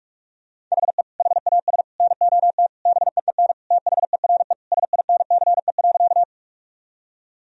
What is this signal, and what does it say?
Morse code, heard as a keyed tone, about 41 wpm, 700 Hz. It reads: HE HAS NOT BEEN THERE SINCE.
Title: Sentences from Top 1000 Words - Review Part 1 - 45wpm